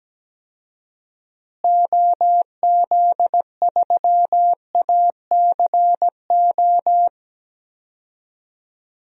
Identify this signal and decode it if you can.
Morse code, heard as a keyed tone, about 17 wpm, 700 Hz. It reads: OZ3ACO
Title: DXCC-100 Call Signs - 6 Characters - Repeated 3x - 17wpm